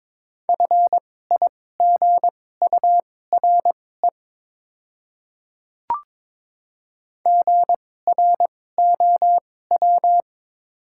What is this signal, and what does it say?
Morse code, heard as a keyed tone, about 22 wpm, spaced as for 16 wpm, 700 Hz. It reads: FIGURE GROW
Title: 2x Character Spacing - Top 500 Words - 22wpm